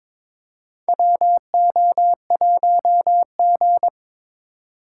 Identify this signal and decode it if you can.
Morse code, heard as a keyed tone, about 21 wpm, 700 Hz. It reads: WO1G
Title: USA Call Signs - 4 Characters - 8x Character Spacing - 22wpm